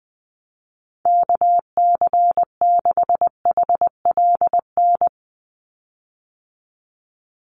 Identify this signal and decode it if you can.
Morse code, heard as a keyed tone, about 20 wpm, 700 Hz. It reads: KC6HLN
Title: USA Call Signs 20wpm